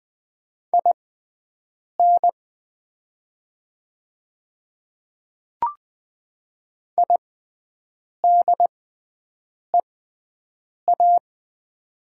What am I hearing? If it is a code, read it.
Morse code, heard as a keyed tone, about 20 wpm, spaced as for 7 wpm, 700 Hz. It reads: IN IDEA
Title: 6x Character Spacing - Top 300 Words - 20wpm